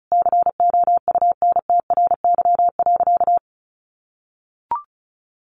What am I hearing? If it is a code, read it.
Morse code, heard as a keyed tone, about 34 wpm, 700 Hz. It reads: COUNTRY.
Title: Sentences from Top 1200 Words 35wpm